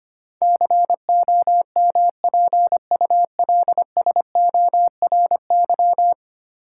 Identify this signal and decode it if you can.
Morse code, heard as a keyed tone, about 25 wpm, 700 Hz. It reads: COMPULSORY
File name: Ten Letter Words - 25wpm